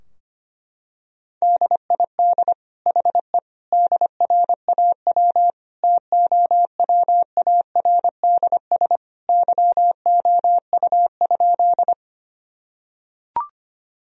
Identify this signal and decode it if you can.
Morse code, heard as a keyed tone, about 25 wpm, 700 Hz. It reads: DID HE DRAW TOWARDS YOU?
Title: Sentences from Top 1000 Words - Review Part 2 - 25wpm